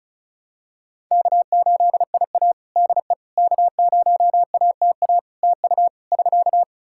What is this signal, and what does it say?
Morse code, heard as a keyed tone, about 34 wpm, 700 Hz. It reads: K8IA DE K0ATA TU <SK>
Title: Practice QSO - Exchange 4 - Ending - 35wpm